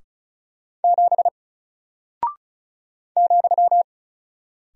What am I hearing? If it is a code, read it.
Morse code, heard as a keyed tone, about 36 wpm, 700 Hz. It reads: 7 ,
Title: Single Character (Letter:Number:Punctuation:Prosign) Alt 35wpm